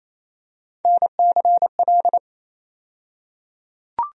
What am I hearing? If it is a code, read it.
Morse code, heard as a keyed tone, about 28 wpm, 700 Hz. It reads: NCL
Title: USA Call Sign Postfixes - 3 Letters 28wpm